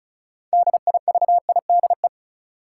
Morse code, heard as a keyed tone, about 36 wpm, 700 Hz. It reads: DIVIDE
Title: Sentences from Top 1600 Words - New Word Drill - 35wpm